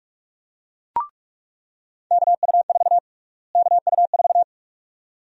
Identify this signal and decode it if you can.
Morse code, heard as a keyed tone, about 41 wpm, 700 Hz. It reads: KU4 KU4
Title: USA Call Sign Prefixes - Repeated 2x - 45wpm